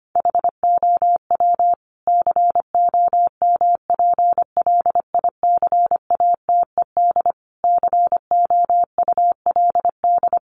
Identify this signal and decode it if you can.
Morse code, heard as a keyed tone, about 25 wpm, 700 Hz. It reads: HOW COMPLICATED COULD
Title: Sentences from Top 1200 Words 25wpm